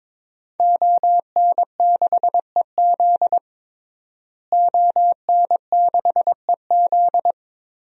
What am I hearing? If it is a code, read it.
Morse code, heard as a keyed tone, about 22 wpm, 700 Hz. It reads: ON6EZ ON6EZ
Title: DXCC-100 Call Signs - 5 Characters - Repeated 3x - 22wpm